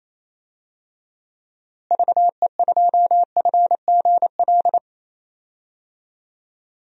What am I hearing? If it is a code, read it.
Morse code, heard as a keyed tone, about 28 wpm, 700 Hz. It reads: VE2FGL
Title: DXCC-100 Call Signs - 6 Characters - Repeated 3x - 28wpm